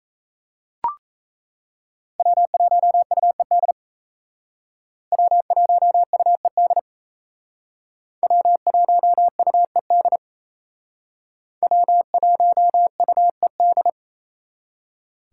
Morse code, heard as a keyed tone, about 38 wpm, 700 Hz. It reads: W1UED W1UED W1UED W1UED